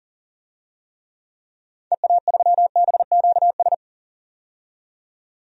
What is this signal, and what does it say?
Morse code, heard as a keyed tone, about 38 wpm, 700 Hz. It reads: EA3BQS
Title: DXCC-100 Call Signs - 6 Characters - 40wpm